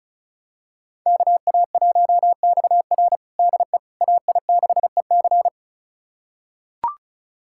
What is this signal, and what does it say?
Morse code, heard as a keyed tone, about 36 wpm, 700 Hz. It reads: KA1XR DE AI6EC
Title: USA Callsign DE Callsign - 5 Characters - 35wpm